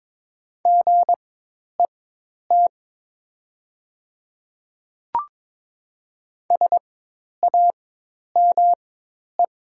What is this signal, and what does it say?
Morse code, heard as a keyed tone, about 22 wpm, spaced as for 10 wpm, 700 Hz. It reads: GET SAME
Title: 4x Character Spacing - Top 200 Words - 22wpm